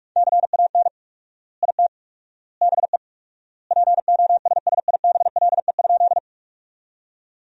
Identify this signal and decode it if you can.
Morse code, heard as a keyed tone, about 46 wpm, 700 Hz. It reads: CAN IT BE POSSIBLE?
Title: 4x Word-Spacing - Sentences from Top 500 Words - 45wpm